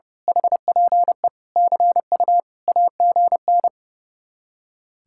Morse code, heard as a keyed tone, about 29 wpm, 700 Hz. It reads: HPE CU AGN
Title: Speed-Racing Common QSO Phrases - 30wpm